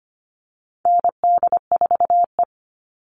Morse code, heard as a keyed tone, about 25 wpm, 700 Hz. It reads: ND4E
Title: All Cut Numbers - 4-Digit 25wpm